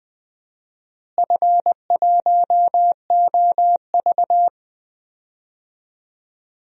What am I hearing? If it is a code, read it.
Morse code, heard as a keyed tone, about 19 wpm, 700 Hz. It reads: F1OV
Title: DXCC-100 Call Signs - 4 Characters - Repeated 3x - 20wpm